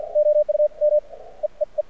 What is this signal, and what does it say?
Morse code, heard as a keyed tone, about 53 wpm, spaced as for 26 wpm, 600 Hz. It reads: OU ME E E
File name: archived / CRSnet